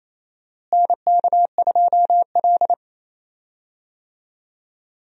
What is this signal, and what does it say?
Morse code, heard as a keyed tone, about 28 wpm, 700 Hz. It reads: NK2L
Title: USA Call Signs - 4 Characters - Repeated 3x - 28wpm